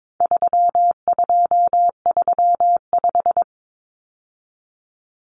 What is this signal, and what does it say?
Morse code, heard as a keyed tone, about 22 wpm, 700 Hz. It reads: 3235
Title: Numbers - 4-Digit 22wpm